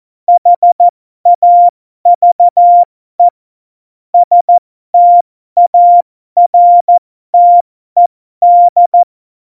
Morse code, heard as a keyed tone, about 14 wpm, spaced as for 12 wpm, 700 Hz. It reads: HAVE STARTED
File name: Morsekurs / Witze